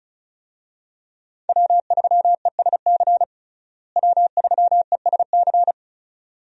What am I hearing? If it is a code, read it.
Morse code, heard as a keyed tone, about 36 wpm, 700 Hz. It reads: W3ESC W3ESC
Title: USA Call Signs - 5 Characters - Repeated 2x - 35wpm